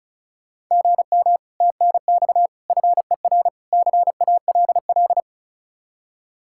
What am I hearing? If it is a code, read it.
Morse code, heard as a keyed tone, about 39 wpm, 700 Hz. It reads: GM TNX FER CALL
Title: Speed-Racing Common QSO Phrases - 35wpm